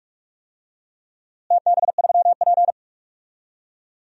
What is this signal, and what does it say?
Morse code, heard as a keyed tone, about 46 wpm, 700 Hz. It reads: TB3P
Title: DXCC-100 Call Signs - 4 Characters - Repeated 3x - 45wpm